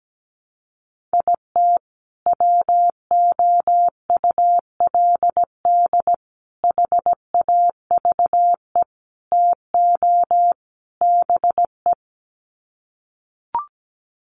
Morse code, heard as a keyed tone, about 17 wpm, 700 Hz. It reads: IT WOULD HAVE TO BE
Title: Sets of 5 Words Encore 17wpm